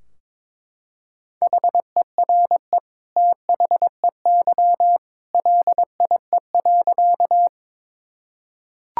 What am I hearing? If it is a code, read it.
Morse code, heard as a keyed tone, about 22 wpm, 700 Hz. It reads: HERE THEY LIE.
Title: Sentences from Top 1600 Words 22wpm